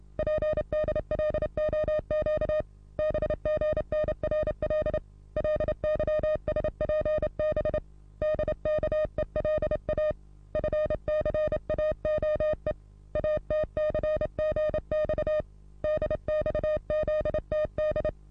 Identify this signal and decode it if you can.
Morse code, heard as a keyed tone, about 31 wpm, 610 Hz. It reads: PDLOQ BGNRL LYSPB DKELA FCAOE ATCGX DXZTD
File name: repos / Morsekurs / DJ4UF Morsen27